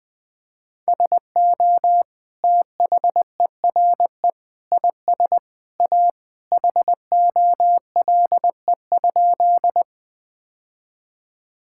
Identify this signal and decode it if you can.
Morse code, heard as a keyed tone, about 20 wpm, 700 Hz. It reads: SO THERE IS A HOLE?
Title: Sentences from Top 1600 Words 20wpm